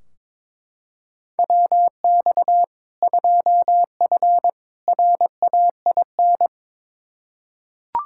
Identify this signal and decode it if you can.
Morse code, heard as a keyed tone, about 22 wpm, 700 Hz. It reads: WX 2F RAIN
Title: K1USN SST QSO Practice 22wpm